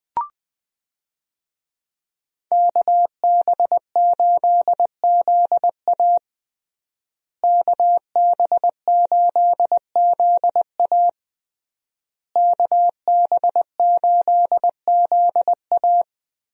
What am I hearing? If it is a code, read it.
Morse code, heard as a keyed tone, about 20 wpm, 700 Hz. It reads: KB8ZA KB8ZA KB8ZA
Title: USA Call Signs - 5 Characters - Repeated 3x - 20wpm